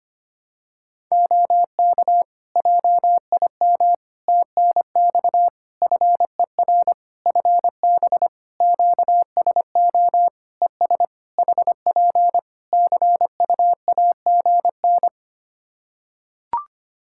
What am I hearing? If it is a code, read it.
Morse code, heard as a keyed tone, about 25 wpm, 700 Hz. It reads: OK JIM TNX FER FB QSO ES HP CUAGN
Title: Practice QSO - Exchange 4 - Ending - 25wpm